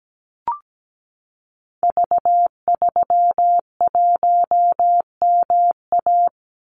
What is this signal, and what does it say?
Morse code, heard as a keyed tone, about 17 wpm, 700 Hz. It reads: V31MA
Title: Call Signs - 17wpm